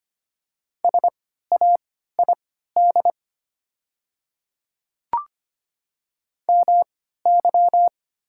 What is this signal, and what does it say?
Morse code, heard as a keyed tone, about 25 wpm, spaced as for 14 wpm, 700 Hz. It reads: SAID MY
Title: 3x Character Spacing - Top 100 Words - 25wpm